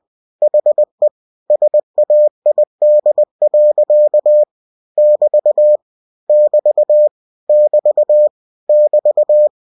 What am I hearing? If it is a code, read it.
Morse code, heard as a keyed tone, about 20 wpm, 600 Hz. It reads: HE SAID. <BT> <BT> <BT> <BT>
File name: cw-books / winnie-the-pooh / winnie-20wpm-page-0110